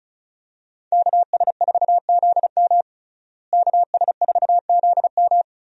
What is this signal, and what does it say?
Morse code, heard as a keyed tone, about 36 wpm, 700 Hz. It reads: KS4ZM KS4ZM
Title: USA Call Signs - 5 Characters - Repeated 2x - 35wpm